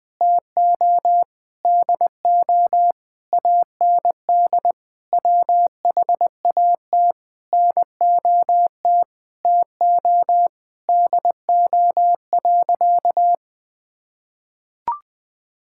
Morse code, heard as a keyed tone, about 20 wpm, 700 Hz. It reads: TO DO AND WHAT NOT TO DO.
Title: Sentences with 8-Word Phrases - 20wpm